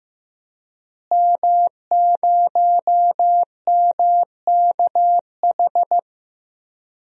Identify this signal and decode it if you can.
Morse code, heard as a keyed tone, about 15 wpm, 700 Hz. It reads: M0MKH
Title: DXCC-100 Call Signs - 5 Characters - Repeated 2x - 15wpm